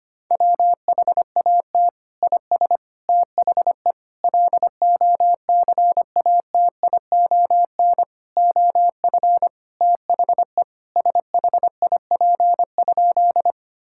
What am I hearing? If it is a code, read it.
Morse code, heard as a keyed tone, about 25 wpm, 700 Hz. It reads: WHAT IS THE LOCATION OF THE SHIP?
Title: Sentences from Top 1000 Words - Review Part 6 - 25wpm